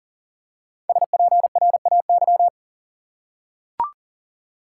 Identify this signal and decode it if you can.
Morse code, heard as a keyed tone, about 40 wpm, 700 Hz. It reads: SPRAY
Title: Five Letter Words - 40wpm